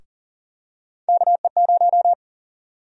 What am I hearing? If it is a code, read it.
Morse code, heard as a keyed tone, about 38 wpm, 700 Hz. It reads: KE0
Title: USA Call Sign Prefixes - Repeated 2x - 40wpm